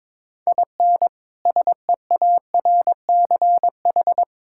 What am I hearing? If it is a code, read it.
Morse code, heard as a keyed tone, about 22 wpm, 700 Hz. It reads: IN SEARCH
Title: Sentences from Top 1000 Words - Review Part 5 - 22wpm